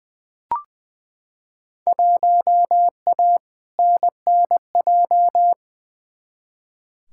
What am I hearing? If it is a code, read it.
Morse code, heard as a keyed tone, about 20 wpm, 700 Hz. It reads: 1A NNJ
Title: Field Day Exchanges Encore - 20wpm